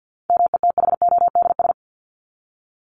Morse code, heard as a keyed tone, about 50 wpm, 700 Hz. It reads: METHODS
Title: ES:S Suffix - Top 100 Words with Base Words 50wpm